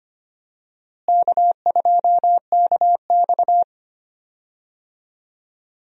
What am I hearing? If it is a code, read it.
Morse code, heard as a keyed tone, about 25 wpm, 700 Hz. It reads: K2KX
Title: USA Call Signs - 4 Characters - Repeated 3x - 25wpm